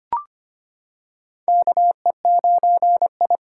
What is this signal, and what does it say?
Morse code, heard as a keyed tone, about 25 wpm, 700 Hz. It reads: KE9I
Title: K1USN SST QSO Practice - 25wpm